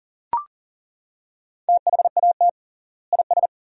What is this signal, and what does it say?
Morse code, heard as a keyed tone, about 40 wpm, 700 Hz. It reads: THAT IS
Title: 3x Word-Spacing - Sentences from Top 500 Words - 40wpm